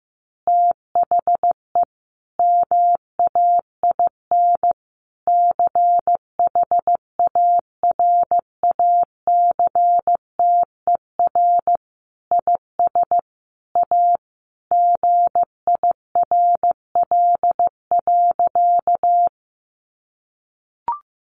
Morse code, heard as a keyed tone, about 15 wpm, 700 Hz. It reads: THE MAIN CHARACTER IS A GIRL.
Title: Additional Easy Phrases 15wpm